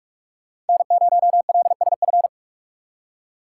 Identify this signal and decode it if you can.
Morse code, heard as a keyed tone, about 59 wpm, 700 Hz. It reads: N0LSF